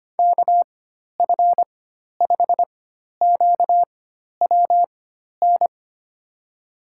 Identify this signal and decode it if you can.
Morse code, heard as a keyed tone, about 25 wpm, spaced as for 12 wpm, 700 Hz. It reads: KF5QWN
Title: USA Call Signs - 6 Characters - 4x Character Spacing - 25wpm